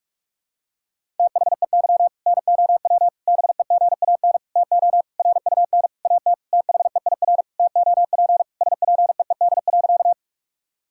Morse code, heard as a keyed tone, about 41 wpm, 700 Hz. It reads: THEY NOW BEGAN TO RUN AT THEIR TOP SPEED.
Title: Sentences from Top 1000 Words - Review Part 5 - 45wpm